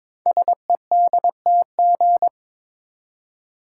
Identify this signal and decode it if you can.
Morse code, heard as a keyed tone, about 22 wpm, 700 Hz. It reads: SEDTG
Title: Code Groups - 5 Letter:Numbers - 22wpm